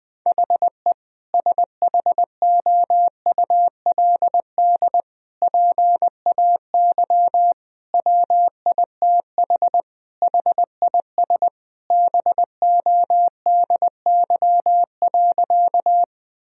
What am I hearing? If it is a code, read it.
Morse code, heard as a keyed tone, about 21 wpm, 700 Hz. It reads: HE SHOULD PAY WITH HIS BODY.
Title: Sentences from Top 1000 Words - Review Part 2 - 20wpm